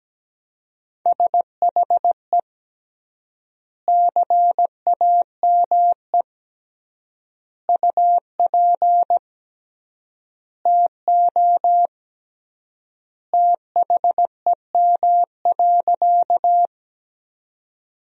Morse code, heard as a keyed tone, about 17 wpm, 700 Hz. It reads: SHE CAME UP TO THEM.
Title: 3x Word-Spacing - Sentences from Top 200 Words - 17wpm